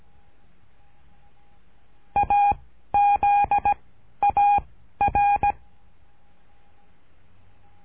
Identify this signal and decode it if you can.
Morse code, heard as a keyed tone, about 17 wpm, 810 Hz. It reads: A Z A R
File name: cursos / CURSO DE TELEGRAFIA / Telegrafia - Trilha 13